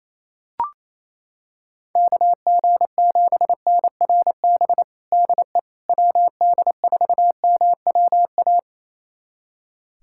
Morse code, heard as a keyed tone, about 28 wpm, 700 Hz. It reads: KG7NRB DE WD4MWA
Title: USA Callsign DE Callsign 28wpm